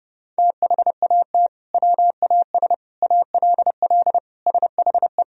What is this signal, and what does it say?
Morse code, heard as a keyed tone, about 30 wpm, 700 Hz. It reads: THAT WAS ALL SHE
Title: Sentences from Top 100 Words and Abbreviations 30wpm